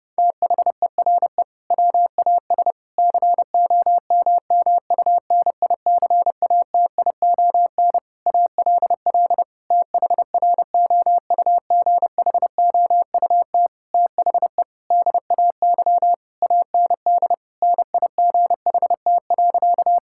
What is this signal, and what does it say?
Morse code, heard as a keyed tone, about 30 wpm, 700 Hz. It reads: THERE WAS COMMUNICATION ALL THROUGHOUT THE DAY AND NIGHT.
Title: Sentences from Top 1000 Words - Review Part 6 - 30wpm